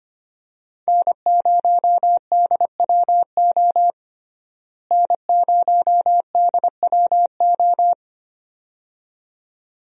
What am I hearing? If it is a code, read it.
Morse code, heard as a keyed tone, about 25 wpm, 700 Hz. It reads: N0DWO N0DWO
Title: USA Call Signs - 5 Characters - Repeated 2x - 25wpm